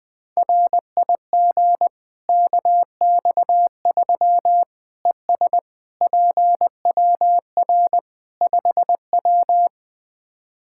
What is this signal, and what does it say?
Morse code, heard as a keyed tone, about 20 wpm, 700 Hz. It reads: RIG KX3 ES PWR 5W
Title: Practice QSO - Exchange 2 - Info - 20wpm